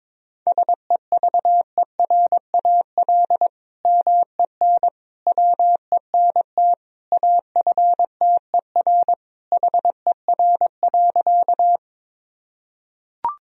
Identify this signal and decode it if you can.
Morse code, heard as a keyed tone, about 22 wpm, 700 Hz. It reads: SEVERAL MEN WENT AFTER HER.
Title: Sentences from Top 1000 Words - Review Part 2 - 22wpm